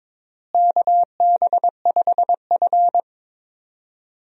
Morse code, heard as a keyed tone, about 22 wpm, 700 Hz. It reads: KB5F
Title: USA Call Signs - 4 Characters - Repeated 3x - 22wpm